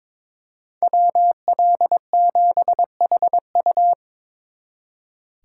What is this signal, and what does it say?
Morse code, heard as a keyed tone, about 25 wpm, 700 Hz. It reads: WL7HU
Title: Speed Racing - USA Call Signs 22wpm